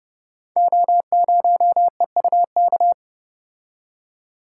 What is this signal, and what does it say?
Morse code, heard as a keyed tone, about 30 wpm, 700 Hz. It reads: O0EUK
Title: Code Groups - 5 Letter:Numbers - 30wpm